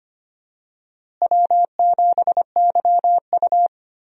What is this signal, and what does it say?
Morse code, heard as a keyed tone, about 25 wpm, 700 Hz. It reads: W7YU
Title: USA Call Signs - 4 Characters - Repeated 3x - 25wpm